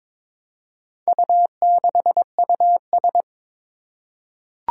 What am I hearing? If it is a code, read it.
Morse code, heard as a keyed tone, about 22 wpm, 700 Hz. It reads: U6US
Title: Four Challenging Letter-Number Combinations 22wpm